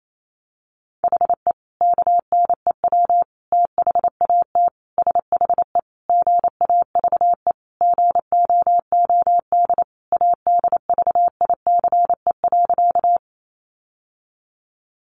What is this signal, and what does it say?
Morse code, heard as a keyed tone, about 28 wpm, 700 Hz. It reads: HE KNEW THAT SHE GAVE GOOD ADVICE.
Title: Sentences from Top 1100 Words 28wpm